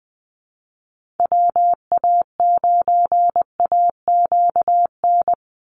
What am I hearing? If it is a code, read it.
Morse code, heard as a keyed tone, about 20 wpm, 700 Hz. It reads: WA9AQN
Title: Call Signs - 20wpm